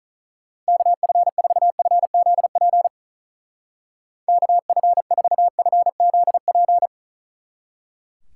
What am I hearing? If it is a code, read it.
Morse code, heard as a keyed tone, about 46 wpm, 700 Hz. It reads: KF4FZP KF4FZP